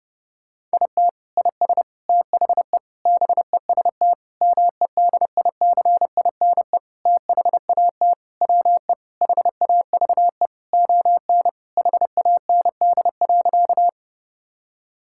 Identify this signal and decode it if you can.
Morse code, heard as a keyed tone, about 32 wpm, 700 Hz. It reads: IT IS THE BEST MEDICINE THAT WE HAVE ON HAND.
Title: Sentences from Top 1000 Words - Review Part 6 - 30wpm